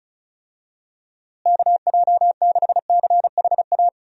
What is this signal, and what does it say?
Morse code, heard as a keyed tone, about 36 wpm, 700 Hz. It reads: KJ6CHA
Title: USA Call Signs - 6 Characters - Repeated 2x - 35wpm